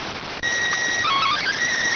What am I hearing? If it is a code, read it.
SSTV (slow-scan television), PD 290.